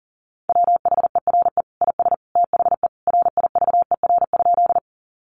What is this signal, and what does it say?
Morse code, heard as a keyed tone, about 40 wpm, 700 Hz. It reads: WHERE IS THE RIVER?
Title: Sentences from Top 300 Words 40wpm